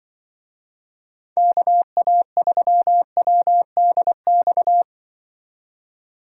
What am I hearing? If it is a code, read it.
Morse code, heard as a keyed tone, about 24 wpm, 700 Hz. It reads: KA3WDX
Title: Speed Racing - USA 2x3 Call Signs 20wpm